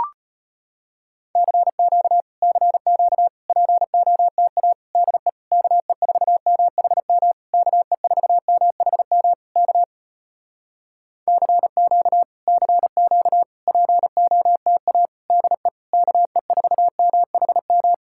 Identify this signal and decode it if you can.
Morse code, heard as a keyed tone, about 35 wpm, 700 Hz. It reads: CQ CQ POTA DE KE4MHM KE4MHM K CQ CQ POTA DE KE4MHM